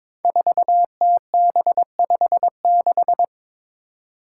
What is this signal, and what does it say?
Morse code, heard as a keyed tone, about 22 wpm, 700 Hz. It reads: 4TB56
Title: Five Challenging Letter-Number Combinations 22wpm